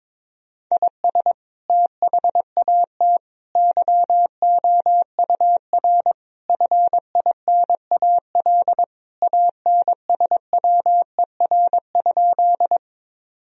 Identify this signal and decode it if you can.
Morse code, heard as a keyed tone, about 22 wpm, 700 Hz. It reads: IS THAT YOUR FINAL ANSWER?
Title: Sentences from Top 1000 Words - Review Part 2 - 22wpm